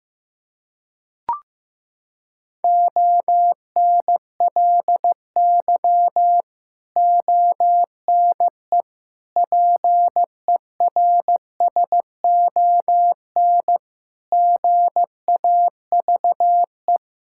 Morse code, heard as a keyed tone, about 15 wpm, 700 Hz. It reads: ONLY ONE PERSON GAVE
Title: Sentences from Top 1000 Words - Review Part 6 - 15wpm